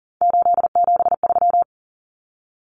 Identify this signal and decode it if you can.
Morse code, heard as a keyed tone, about 40 wpm, 700 Hz. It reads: 873
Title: Common Cut Numbers - 3-Digit 40wpm